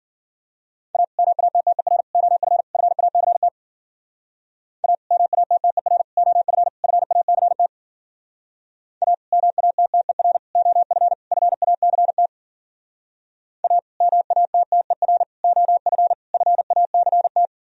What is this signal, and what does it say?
Morse code, heard as a keyed tone, about 52 wpm, 700 Hz. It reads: A MATTER OF FACT A MATTER OF FACT A MATTER OF FACT A MATTER OF FACT